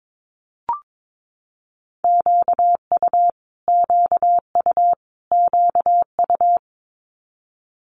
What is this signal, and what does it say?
Morse code, heard as a keyed tone, about 22 wpm, 700 Hz. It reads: QU QU QU
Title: Multi-letter Phonemes Repeated 3x 22wpm